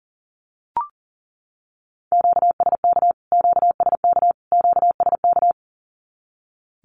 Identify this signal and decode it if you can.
Morse code, heard as a keyed tone, about 40 wpm, 700 Hz. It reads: QSK QSK QSK
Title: Q-Codes Repeated 3x 40wpm